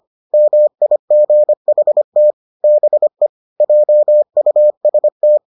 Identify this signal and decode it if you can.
Morse code, heard as a keyed tone, about 25 wpm, 600 Hz. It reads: MIGHT BE JUST